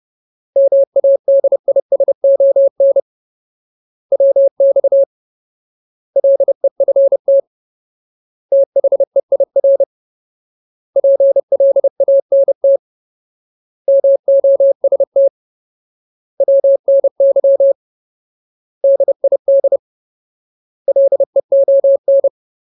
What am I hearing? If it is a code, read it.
Morse code, heard as a keyed tone, about 30 wpm, 550 Hz. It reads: MADISON WX LEFT THEIR PLANT MOST WNY DID LEON
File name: Ultimate Word List / 30WPM Ultimate